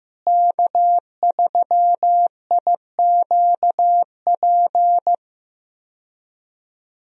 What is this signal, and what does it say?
Morse code, heard as a keyed tone, about 15 wpm, 700 Hz. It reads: K3IQP
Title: USA Call Signs - 5 Characters - Repeated 3x - 15wpm